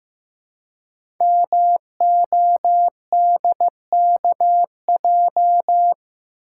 Morse code, heard as a keyed tone, about 15 wpm, 700 Hz. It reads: MODKJ
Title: Code Groups - 5 Letter:Numbers - 15wpm